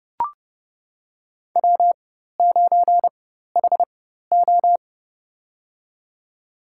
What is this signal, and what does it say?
Morse code, heard as a keyed tone, about 29 wpm, spaced as for 14 wpm, 700 Hz. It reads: W9HO
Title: USA Call Signs - 4 Characters - 4x Character Spacing - 30wpm